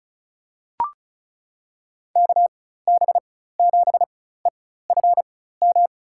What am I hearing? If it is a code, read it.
Morse code, heard as a keyed tone, about 36 wpm, spaced as for 16 wpm, 700 Hz. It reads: KB7EFM
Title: USA Call Signs - 6 Characters - 4x Character Spacing - 35wpm